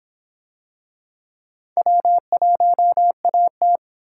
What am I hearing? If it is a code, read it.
Morse code, heard as a keyed tone, about 26 wpm, 700 Hz. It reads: W1AT